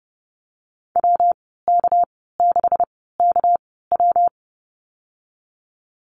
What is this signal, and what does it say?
Morse code, heard as a keyed tone, about 30 wpm, spaced as for 17 wpm, 700 Hz. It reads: WK6KW
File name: Lesson 37g - Letter x - Call Signs 30wpm